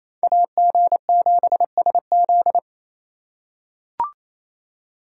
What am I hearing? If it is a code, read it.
Morse code, heard as a keyed tone, about 28 wpm, 700 Hz. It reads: AG7SZ
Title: USA Call Signs - 5 Characters - 4x Character Spacing - 28wpm